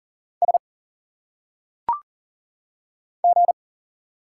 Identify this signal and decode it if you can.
Morse code, heard as a keyed tone, about 41 wpm, 700 Hz. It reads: S G
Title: Single Letters - Phonetic Pronunciation 40wpm